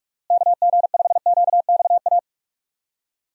Morse code, heard as a keyed tone, about 42 wpm, 700 Hz. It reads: KG5QXA
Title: USA Call Signs - 6 Characters - Repeated 3x - 45wpm